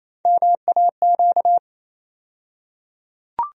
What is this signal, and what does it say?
Morse code, heard as a keyed tone, about 28 wpm, 700 Hz. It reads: MAQ
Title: USA Call Sign Postfixes - 3 Letters 28wpm